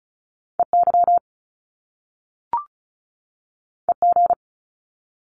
Morse code, heard as a keyed tone, about 36 wpm, 700 Hz. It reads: EY EG
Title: Top Two Letter Combinations in Words 35wpm